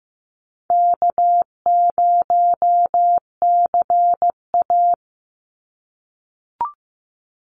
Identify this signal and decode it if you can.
Morse code, heard as a keyed tone, about 15 wpm, 700 Hz. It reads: K0CA
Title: Easy Call Signs - 15wpm